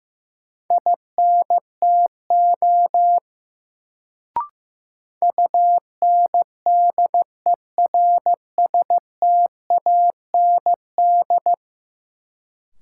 Morse code, heard as a keyed tone, about 15 wpm, 700 Hz. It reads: INTO UNDERSTAND
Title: Sentences from Top 600 Words - New Word Drill - 15wpm